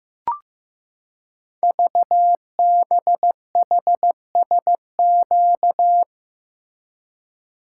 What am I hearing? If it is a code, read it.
Morse code, heard as a keyed tone, about 15 wpm, 700 Hz. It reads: VBHSQ
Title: Code Groups - 5 Letter:Numbers - 15wpm